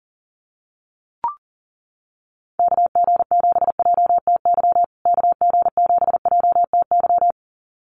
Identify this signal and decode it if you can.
Morse code, heard as a keyed tone, about 40 wpm, 700 Hz. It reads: KG7JTY KG7JTY
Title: USA Call Signs Repeated 2x 40wpm